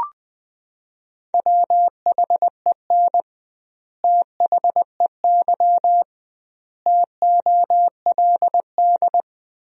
Morse code, heard as a keyed tone, about 21 wpm, 700 Hz. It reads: WHEN THEY TOLD
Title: 2x Word-Spacing - Sentences from Top 400 Words - 20wpm